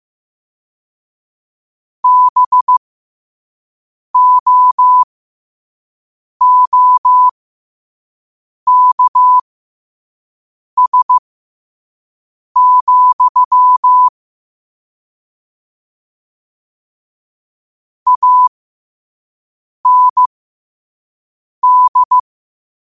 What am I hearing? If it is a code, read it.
Morse code, heard as a keyed tone, about 15 wpm, spaced as for 5 wpm, 1000 Hz. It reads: BOOKS, AND
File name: qotdfiles / 5wpm